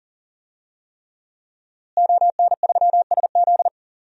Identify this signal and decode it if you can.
Morse code, heard as a keyed tone, about 40 wpm, 700 Hz. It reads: ON3SZ
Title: DXCC-100 Call Signs - 5 Characters - Repeated 3x - 40wpm